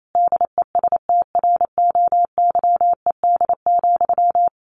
Morse code, heard as a keyed tone, about 28 wpm, 700 Hz. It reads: DESTROYED,
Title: A Sound Of Thunder 28wpm